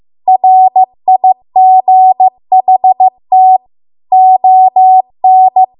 Morse code, heard as a keyed tone, about 15 wpm, 750 Hz. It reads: RIGHT ON
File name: Archive / 15 WPM / 15WPM